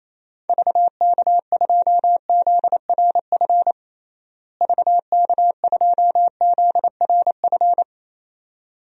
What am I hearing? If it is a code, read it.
Morse code, heard as a keyed tone, about 28 wpm, 700 Hz. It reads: VK2ZRF VK2ZRF
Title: DXCC-100 Call Signs - 6 Characters - Repeated 2x - 28wpm